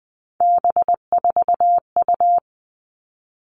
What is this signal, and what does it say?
Morse code, heard as a keyed tone, about 20 wpm, 700 Hz. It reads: B4U
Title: All Cut Numbers - 3-Digit 20wpm